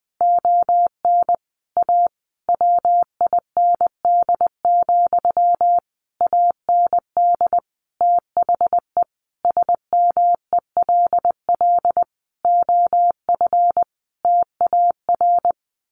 Morse code, heard as a keyed tone, about 20 wpm, 700 Hz. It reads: ON A WIND, AND THE SMELL OF TAR
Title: A Sound Of Thunder 20wpm